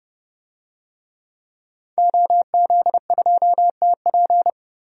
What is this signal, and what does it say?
Morse code, heard as a keyed tone, about 28 wpm, 690 Hz. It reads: OZ2TP
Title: DXCC-100 Call Signs - 5 Characters - 30wpm